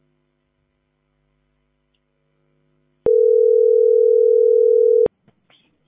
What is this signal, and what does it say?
Ringback tone (North America), 440 + 480 Hz.